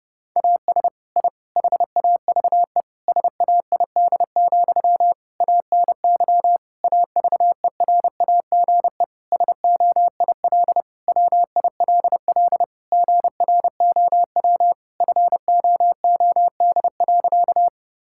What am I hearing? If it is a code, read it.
Morse code, heard as a keyed tone, about 30 wpm, 700 Hz. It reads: AS I HAVE SAID, ANY AVERAGE SOIL WILL GROW FOOD.
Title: Sentences from Top 1000 Words - Review Part 6 - 30wpm